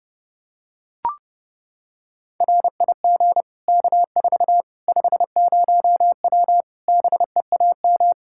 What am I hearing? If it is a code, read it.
Morse code, heard as a keyed tone, about 30 wpm, 700 Hz. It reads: RIG K4 50W BEAM
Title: Practice QSO - Exchange 2 - Info - 30wpm